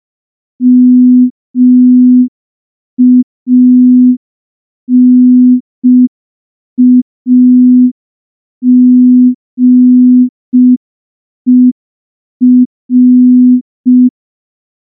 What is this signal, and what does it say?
Morse code, heard as a keyed tone, about 5 wpm, 250 Hz. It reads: MANAGER
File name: morse / morse-training-05WPM